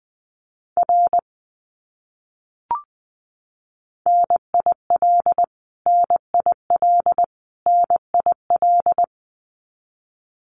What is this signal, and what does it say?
Morse code, heard as a keyed tone, about 20 wpm, 700 Hz. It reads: R NIL NIL NIL
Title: Abbreviations Repeated 3x - 20wpm